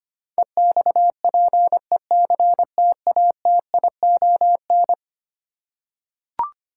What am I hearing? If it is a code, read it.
Morse code, heard as a keyed tone, about 25 wpm, 700 Hz. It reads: EXPECTATION
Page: morse-code-ninja-practice-25wpm